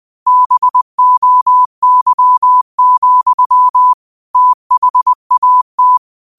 Morse code, heard as a keyed tone, about 20 wpm, 1000 Hz. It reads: BOY, THAT
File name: qotdfiles / 20wpm